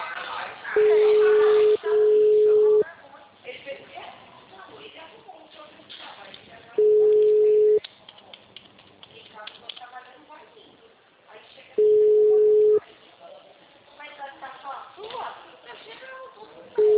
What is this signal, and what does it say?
Ringback tone (Europe, 425 Hz), with speech.